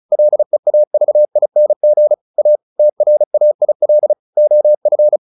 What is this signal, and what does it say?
Morse code, heard as a keyed tone, about 34 wpm, 600 Hz. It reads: LEAVING A TRAIL OF
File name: the old man and the sea 2 35wpm